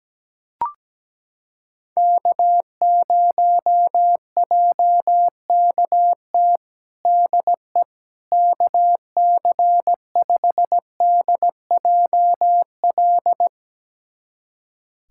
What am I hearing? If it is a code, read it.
Morse code, heard as a keyed tone, about 17 wpm, 700 Hz. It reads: K0JKT DE KC5DJL
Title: USA Callsign DE Callsign 17wpm